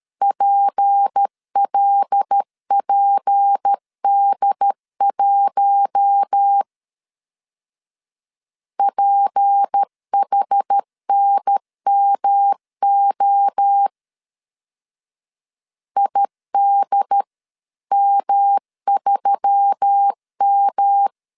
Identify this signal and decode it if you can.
Morse code, heard as a keyed tone, about 13 wpm, 800 Hz. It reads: PLPD1 PHNMO IDM3M
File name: LEZIONI / 6 PKX0 / 6 P 12wpm 2033